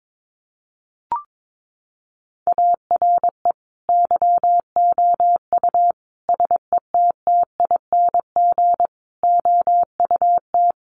Morse code, heard as a keyed tone, about 22 wpm, 700 Hz. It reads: ARE YOU SETTING OUT
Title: Sentences from Top 1300 Words 22wpm